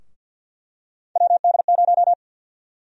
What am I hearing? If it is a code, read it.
Morse code, heard as a keyed tone, about 52 wpm, 700 Hz. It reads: WD0
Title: USA Call Sign Prefixes - 50wpm